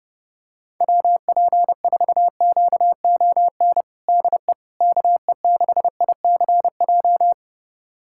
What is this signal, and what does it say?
Morse code, heard as a keyed tone, about 31 wpm, 700 Hz. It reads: WP4QON DE KE6ICJ
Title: USA Callsign DE Callsign 30wpm